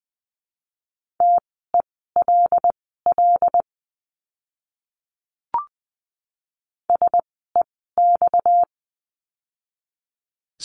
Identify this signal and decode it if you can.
Morse code, heard as a keyed tone, about 20 wpm, spaced as for 14 wpm, 700 Hz. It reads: TELL SEX
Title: 230 Lesson 37f - Letter x - Words 20wpm